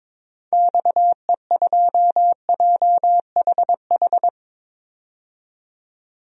Morse code, heard as a keyed tone, about 22 wpm, 700 Hz. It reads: XE2JHH
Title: DXCC-100 Call Signs - 6 Characters - 22wpm